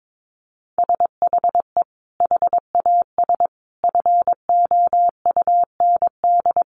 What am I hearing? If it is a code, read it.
Morse code, heard as a keyed tone, about 22 wpm, 700 Hz. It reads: SHE HAS FOUND
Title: Sentences from Top 1500 Words 22wpm